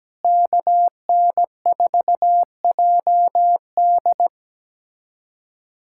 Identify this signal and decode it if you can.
Morse code, heard as a keyed tone, about 17 wpm, 700 Hz. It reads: KN4JD
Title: USA Callsign DE Callsign - 5 Characters - 17wpm